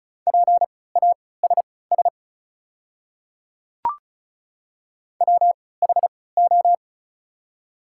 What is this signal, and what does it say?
Morse code, heard as a keyed tone, about 34 wpm, spaced as for 20 wpm, 700 Hz. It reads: PASS WHO
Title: 3x Character Spacing - Top 500 Words - 35wpm